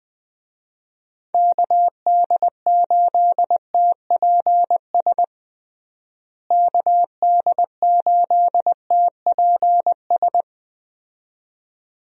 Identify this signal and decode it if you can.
Morse code, heard as a keyed tone, about 20 wpm, 700 Hz. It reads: KD8TPS KD8TPS
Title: USA Call Signs - 6 Characters - Repeated 2x - 20wpm